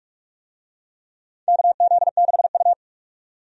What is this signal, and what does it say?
Morse code, heard as a keyed tone, about 46 wpm, 700 Hz. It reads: KZ6U
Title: USA Call Signs - 4 Characters - 45wpm